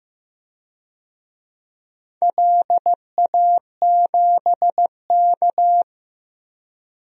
Morse code, heard as a keyed tone, about 15 wpm, 700 Hz. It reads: LA7K
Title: DXCC-100 Call Signs - 4 Characters - 15wpm